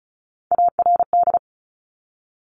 Morse code, heard as a keyed tone, about 36 wpm, 700 Hz. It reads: ARD
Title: Top Three Letter Combinations in Words 35wpm